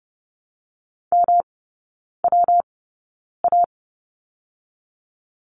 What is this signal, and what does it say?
Morse code, heard as a keyed tone, about 31 wpm, 700 Hz. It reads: M W A
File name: Lesson 19e - Letter w - Three Characters 30wpm